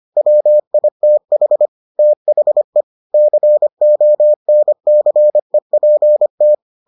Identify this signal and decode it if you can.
Morse code, heard as a keyed tone, about 25 wpm, 600 Hz. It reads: WITH THE CONCEPT